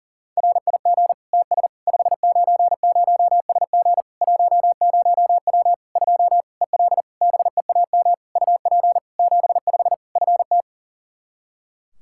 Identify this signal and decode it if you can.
Morse code, heard as a keyed tone, about 40 wpm, 700 Hz. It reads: RIG TS 590SG 10W 2 EL BEAM UP 75 FT
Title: Practice QSO - Exchange 2 - Info - 40wpm